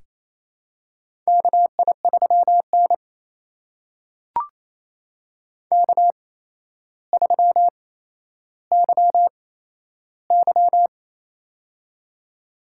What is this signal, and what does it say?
Morse code, heard as a keyed tone, about 28 wpm, spaced as for 12 wpm, 700 Hz. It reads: KI3N K3YY
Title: USA Call Signs - 4 Characters - 8x Character Spacing - 28wpm